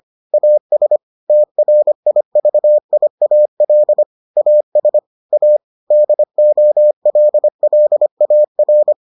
Morse code, heard as a keyed tone, about 25 wpm, 600 Hz. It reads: AS TRIVIAL AS A DOLLAR